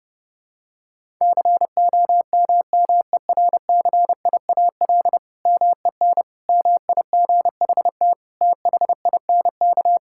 Morse code, heard as a keyed tone, about 30 wpm, 700 Hz. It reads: COMMERCIAL MEN MIGHT THINK
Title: Sentences from Top 1000 Words - Review Part 5 - 30wpm